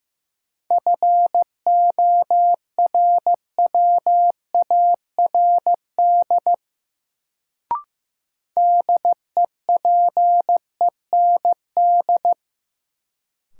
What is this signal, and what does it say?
Morse code, heard as a keyed tone, about 15 wpm, 700 Hz. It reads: FORWARD DEPEND
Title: Sentences from Top 1300 Words - New Word Drill Alt - 15wpm